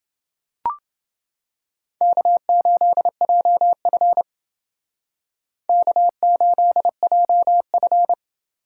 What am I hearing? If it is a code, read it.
Morse code, heard as a keyed tone, about 31 wpm, 700 Hz. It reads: K8JF K8JF